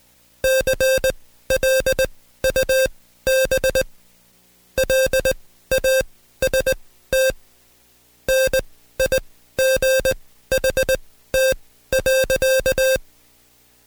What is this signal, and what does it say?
Morse code, heard as a keyed tone, about 20 wpm, spaced as for 14 wpm, 530 Hz. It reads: CLUB LAST NIGHT.
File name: Code Quick / AudioDownld2 / Seg133-Stepper6 13WPM